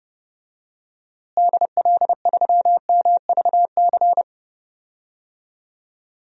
Morse code, heard as a keyed tone, about 29 wpm, 700 Hz. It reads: DL3MVC
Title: DXCC-100 Call Signs - 6 Characters - Repeated 3x - 30wpm